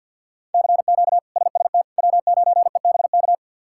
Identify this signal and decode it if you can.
Morse code, heard as a keyed tone, about 51 wpm, 690 Hz. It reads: CQ SST W9EBK
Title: K1USN SST QSO Practice - 50wpm